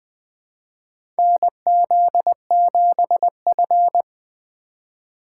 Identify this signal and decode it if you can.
Morse code, heard as a keyed tone, about 20 wpm, 700 Hz. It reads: NZ7F
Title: USA Call Signs - 4 Characters - 3x Character Spacing - 20wpm